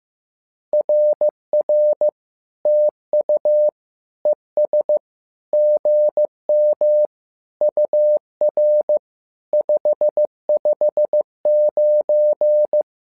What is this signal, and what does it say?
Morse code, heard as a keyed tone, about 15 wpm, 600 Hz. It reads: RR TU ES GM UR 559
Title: SOTA QSO Practice 15wpm